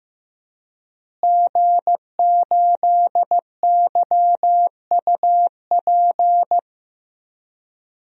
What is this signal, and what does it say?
Morse code, heard as a keyed tone, about 15 wpm, 700 Hz. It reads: G8YUP
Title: DXCC-100 Call Signs - 5 Characters - Repeated 2x - 15wpm